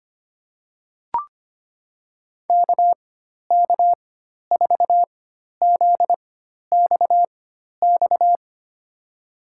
Morse code, heard as a keyed tone, about 25 wpm, spaced as for 12 wpm, 700 Hz. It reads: KK4ZXX
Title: USA Call Signs - 6 Characters - 4x Character Spacing - 25wpm